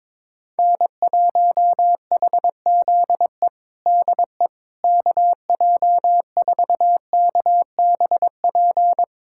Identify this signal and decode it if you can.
Morse code, heard as a keyed tone, about 22 wpm, 700 Hz. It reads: N1HZE DE KJ4KBP
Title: USA Callsign DE Callsign 22wpm